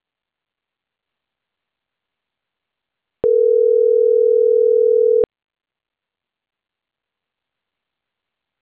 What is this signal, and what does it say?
Ringback tone (North America), 440 + 480 Hz.
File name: CENTRAL CAEA / out agcaea06